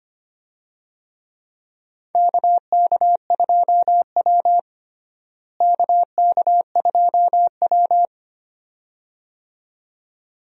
Morse code, heard as a keyed tone, about 25 wpm, 700 Hz. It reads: KK2W KK2W
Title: USA Call Signs - 4 Characters - Repeated 2x - 25wpm